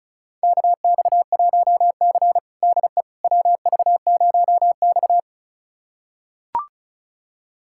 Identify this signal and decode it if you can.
Morse code, heard as a keyed tone, about 36 wpm, 700 Hz. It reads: KX1C DE WV0X
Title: USA Callsign DE Callsign - 4 Characters - 35wpm